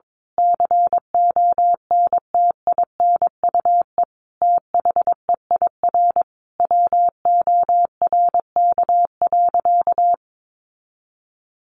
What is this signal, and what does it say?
Morse code, heard as a keyed tone, about 22 wpm, 700 Hz. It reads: CONTINUE THEIR WORK.
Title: Sentences from Top 1100 Words 22wpm